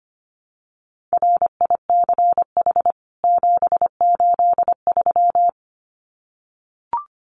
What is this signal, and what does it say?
Morse code, heard as a keyed tone, about 25 wpm, 700 Hz. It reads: RICH 783
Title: CWT Exchange 25wpm